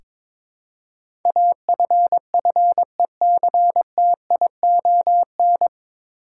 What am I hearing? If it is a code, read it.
Morse code, heard as a keyed tone, about 22 wpm, 700 Hz. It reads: AFFECTION
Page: Morse Code Ninja practice 22wpm